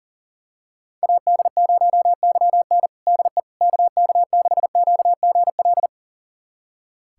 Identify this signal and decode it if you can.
Morse code, heard as a keyed tone, about 37 wpm, 700 Hz. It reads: AD0YN DE KK6QGL
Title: USA Callsign DE Callsign 40wpm